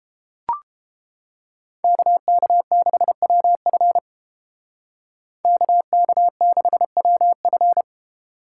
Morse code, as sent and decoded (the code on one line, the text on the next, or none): -.- -.- -.... .-- ..-. / -.- -.- -.... .-- ..-.
KK6WF KK6WF